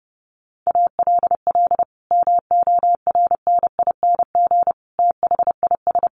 Morse code, heard as a keyed tone, about 30 wpm, 700 Hz. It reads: ALL MORNING THIS
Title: Sentences from Top 1500 Words 30wpm